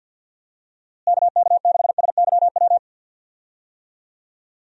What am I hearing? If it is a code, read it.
Morse code, heard as a keyed tone, about 52 wpm, 700 Hz. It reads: KK6SYW
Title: USA Call Signs - 6 Characters - Repeated 3x - 50wpm